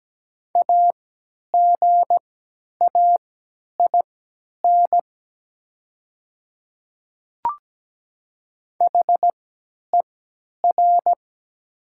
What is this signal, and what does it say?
Morse code, heard as a keyed tone, about 17 wpm, spaced as for 10 wpm, 700 Hz. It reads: AGAIN HER
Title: 3x Character Spacing - Top 500 Words - 17wpm